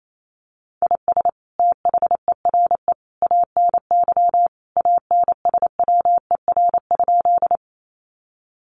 Morse code, heard as a keyed tone, about 28 wpm, 700 Hz. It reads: IS THERE ANY ANSWER?
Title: Sentences from Top 300 Words 28wpm